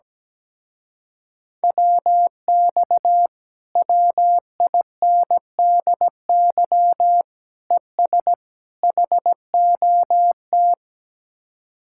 Morse code, heard as a keyed tone, about 17 wpm, 700 Hz. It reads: WX WINDY ES HOT
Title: Speed-Racing Common QSO Phrases - 17wpm